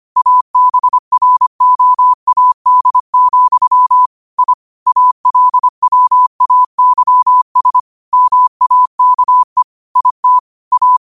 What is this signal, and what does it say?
Morse code, heard as a keyed tone, about 25 wpm, 1000 Hz. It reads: ABROAD, I ALWAYS MAKE IT A
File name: qotdfiles / 25wpm